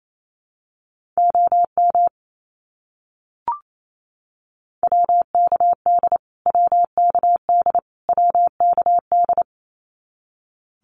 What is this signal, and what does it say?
Morse code, heard as a keyed tone, about 28 wpm, 700 Hz. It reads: OM WKD WKD WKD
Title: Abbreviations Repeated 3x - 28wpm